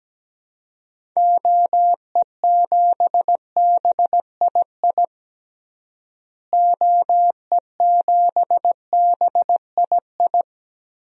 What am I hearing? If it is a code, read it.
Morse code, heard as a keyed tone, about 17 wpm, 700 Hz. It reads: OE7BII OE7BII
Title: DXCC-100 Call Signs - 6 Characters - Repeated 3x - 17wpm